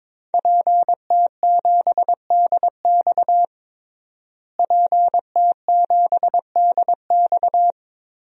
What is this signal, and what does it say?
Morse code, heard as a keyed tone, about 22 wpm, 700 Hz. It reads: PT7DX PT7DX
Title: DXCC-100 Call Signs - 5 Characters - Repeated 3x - 22wpm